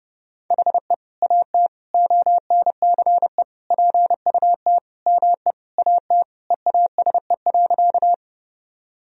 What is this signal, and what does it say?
Morse code, heard as a keyed tone, about 32 wpm, 700 Hz. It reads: HE AT ONCE PUT ME AT EASE.
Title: Sentences from Top 1000 Words - Review Part 1 - 30wpm